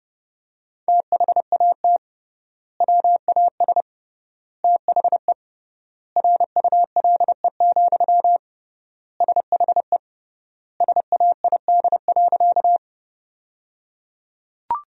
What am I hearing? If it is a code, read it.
Morse code, heard as a keyed tone, about 30 wpm, 700 Hz. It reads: THAT WAS THE RULE, SHE SAID.
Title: 3x Word-Spacing - Sentences from Top 400 Words - 30wpm